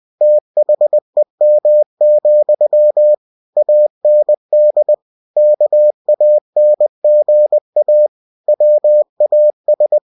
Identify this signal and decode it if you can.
Morse code, heard as a keyed tone, about 20 wpm, 600 Hz. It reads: THEM, AND KANGA WAS
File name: cw-books / winnie-the-pooh / winnie-20wpm-page-0066